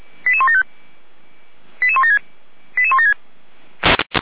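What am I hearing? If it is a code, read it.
Five-tone selective call (ZVEI), heard as quick runs of five tones, tones of about 70 ms. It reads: 88175 88176 88176
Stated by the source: Received on SDR, ZVEI five-tone selective call 88179.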